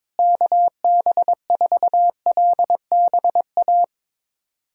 Morse code, heard as a keyed tone, about 22 wpm, 700 Hz. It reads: KB4LBA
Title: USA Callsign DE Callsign 22wpm